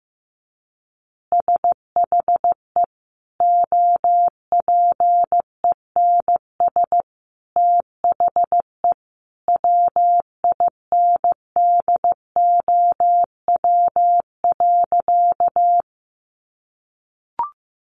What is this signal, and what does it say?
Morse code, heard as a keyed tone, about 15 wpm, 700 Hz. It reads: SHE OPENS THE WINDOW.
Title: Easy Phrases 15wpm